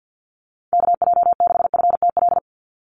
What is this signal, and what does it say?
Morse code, heard as a keyed tone, about 49 wpm, 700 Hz. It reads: KJ6FTL
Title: USA Call Signs 50wpm